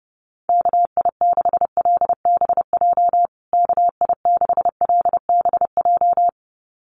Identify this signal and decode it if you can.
Morse code, heard as a keyed tone, about 28 wpm, 700 Hz. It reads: KI6LBJ KI6LBJ
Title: USA Call Signs Repeated 2x 30wpm